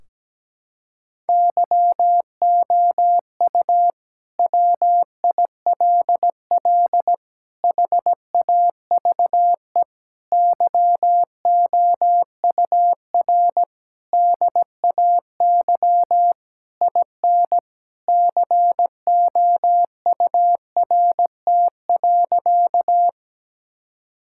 Morse code, heard as a keyed tone, about 17 wpm, 700 Hz. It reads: YOU WILL HAVE YOUR DAY IN COURT.
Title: Sentences from Top 1600 Words 17wpm